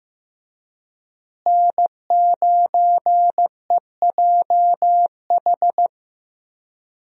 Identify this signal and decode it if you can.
Morse code, heard as a keyed tone, about 15 wpm, 700 Hz. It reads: N9EJH
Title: USA Call Signs - 5 Characters - Repeated 3x - 15wpm